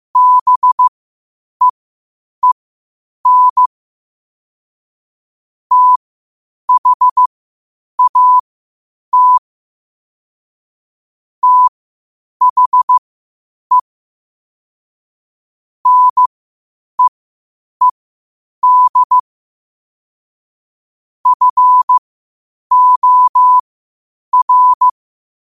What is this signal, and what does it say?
Morse code, heard as a keyed tone, about 15 wpm, spaced as for 8 wpm, 1000 Hz. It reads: BEEN THAT THE NEED FOR
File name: qotdfiles / 8wpm